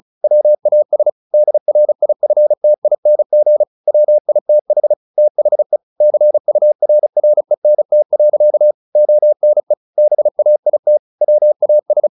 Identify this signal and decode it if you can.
Morse code, heard as a keyed tone, about 34 wpm, 600 Hz. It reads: WAS DRIFTING WITH THE CURRENT. ONE BAIT WAS
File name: the old man and the sea / the old man and the sea 1 35wpm